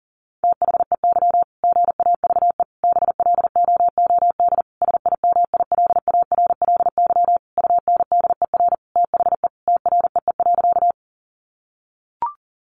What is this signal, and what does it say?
Morse code, heard as a keyed tone, about 40 wpm, 700 Hz. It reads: THEY GAVE BLOOD SIMILARLY UNDER THE TREE.
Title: Sentences from Top 1400 Words 40wpm